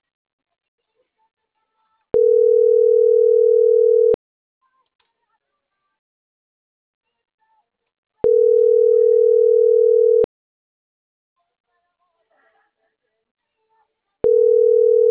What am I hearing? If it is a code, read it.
Ringback tone (North America), 440 + 480 Hz.